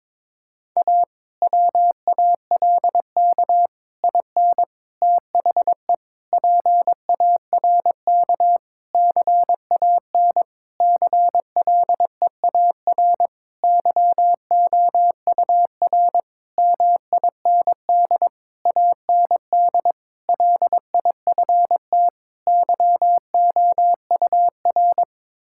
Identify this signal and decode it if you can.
Morse code, heard as a keyed tone, about 22 wpm, 700 Hz. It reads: A WALK IN THE PARK CAN CLEAR YOUR MIND AND LIFT YOUR
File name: Difficult Sentences with Hints - 22wpm